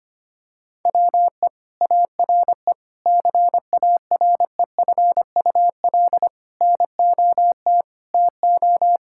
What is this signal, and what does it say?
Morse code, heard as a keyed tone, about 25 wpm, 700 Hz. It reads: WE ARE CAREFUL NOT TO
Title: Sentences from Top 1600 Words 25wpm